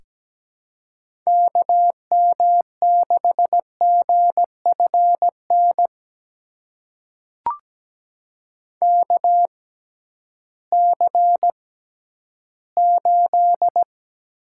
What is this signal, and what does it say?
Morse code, heard as a keyed tone, about 17 wpm, spaced as for 11 wpm, 700 Hz. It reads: KM6GFN KC8
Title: USA Call Signs - 6 Characters - 6x Character Spacing - 17wpm